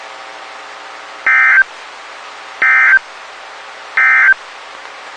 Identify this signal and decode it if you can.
Emergency Alert System (SAME) tones, heard as harsh buzzing bursts. It is heard from 1.2 to 4.4 s.
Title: WXK91 NOAA Weather Radio